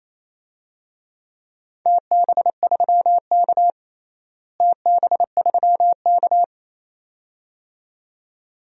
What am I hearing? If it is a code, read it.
Morse code, heard as a keyed tone, about 28 wpm, 700 Hz. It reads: TB3K TB3K
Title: DXCC-100 Call Signs - 4 Characters - Repeated 2x - 28wpm